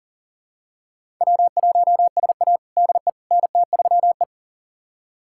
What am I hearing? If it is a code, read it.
Morse code, heard as a keyed tone, about 40 wpm, 700 Hz. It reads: W1SA DE NT3E
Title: USA Callsign DE Callsign - 4 Characters - 40wpm